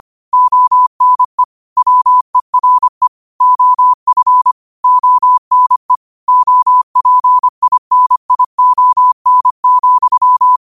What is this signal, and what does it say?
Morse code, heard as a keyed tone, about 25 wpm, 1000 Hz. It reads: ONE WERE OF ONE OPINION,